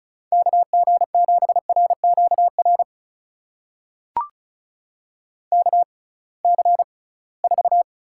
Morse code, heard as a keyed tone, about 36 wpm, spaced as for 23 wpm, 700 Hz. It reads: KG7RQR KC4
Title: USA Call Signs - 6 Characters - 6x Character Spacing - 35wpm